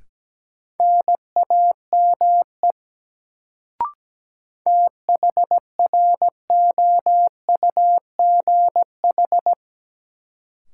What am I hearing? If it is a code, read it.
Morse code, heard as a keyed tone, about 17 wpm, 700 Hz. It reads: NAME THROUGH
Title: Sentences from Top 200 Words - New Word Drill - 17wpm